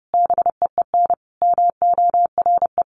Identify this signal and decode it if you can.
Morse code, heard as a keyed tone, about 30 wpm, 700 Hz. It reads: BEEN MORE
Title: Sentences from Top 1300 Words 30wpm